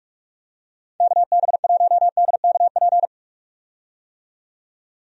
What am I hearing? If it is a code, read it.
Morse code, heard as a keyed tone, about 46 wpm, 700 Hz. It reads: KB1DKP
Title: USA Call Signs - 6 Characters - 45wpm